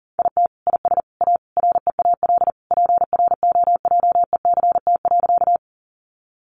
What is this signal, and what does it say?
Morse code, heard as a keyed tone, about 40 wpm, 700 Hz. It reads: IT IS A REAL PROJECT.
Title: Sentences from Top 1400 Words 40wpm